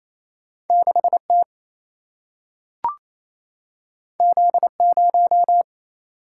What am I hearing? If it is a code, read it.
Morse code, heard as a keyed tone, about 28 wpm, 700 Hz. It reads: 6 T Z 0
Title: Single-Letter Single-Number 28wpm